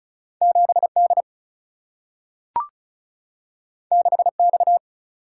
Morse code, heard as a keyed tone, about 36 wpm, 700 Hz. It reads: 7D 6X
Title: Single-Letter Single-Number 35wpm